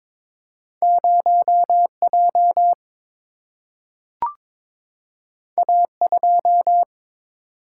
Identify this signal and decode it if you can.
Morse code, heard as a keyed tone, about 22 wpm, 700 Hz. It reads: 0J A2
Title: Single-Letter Single-Number 22wpm